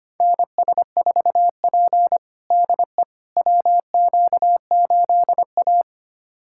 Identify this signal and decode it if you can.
Morse code, heard as a keyed tone, about 25 wpm, 700 Hz. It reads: NS4P DE WQ8A
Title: USA Callsign DE Callsign - 4 Characters - 25wpm